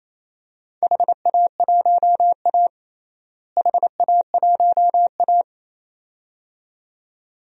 Morse code, heard as a keyed tone, about 27 wpm, 700 Hz. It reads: HA1A HA1A
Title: DXCC-100 Call Signs - 4 Characters - Repeated 2x - 28wpm